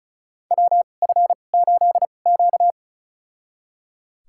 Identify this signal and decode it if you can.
Morse code, heard as a keyed tone, about 36 wpm, spaced as for 25 wpm, 700 Hz. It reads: WF8Q
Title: USA Call Signs - 4 Characters - 2x Character Spacing - 35wpm